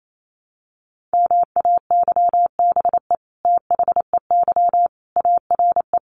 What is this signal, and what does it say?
Morse code, heard as a keyed tone, about 28 wpm, 700 Hz. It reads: MAYBE THEY ARE
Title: Sentences from Top 1000 Words 28wpm